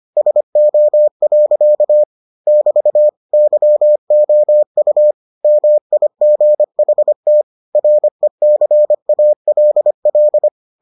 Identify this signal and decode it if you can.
Morse code, heard as a keyed tone, about 25 wpm, 600 Hz. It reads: SO. <BT> YOU MIGHT RECALL